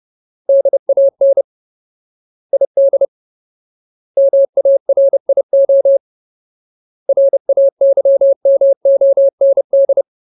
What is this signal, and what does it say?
Morse code, heard as a keyed tone, about 30 wpm, 550 Hz. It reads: DAN ID MARIO RAYMOND
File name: Ultimate Word List / 30WPM Ultimate